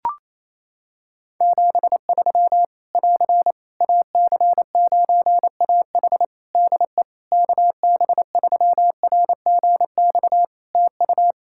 Morse code, heard as a keyed tone, about 27 wpm, 700 Hz. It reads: 73 <AR> AC9AH DE KB3RGX TU
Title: Practice QSO - Exchange 4 - Ending - 28wpm